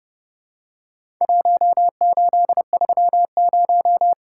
Morse code, heard as a keyed tone, about 30 wpm, 700 Hz. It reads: 1830
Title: Worked All Europe DX Contest - QTC Exchange - 30wpm